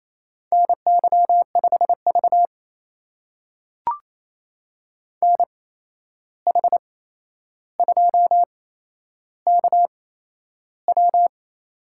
Morse code, heard as a keyed tone, about 28 wpm, spaced as for 11 wpm, 700 Hz. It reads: NY5V NH2KW
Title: USA Call Signs - 8x Character Spacing - 28wpm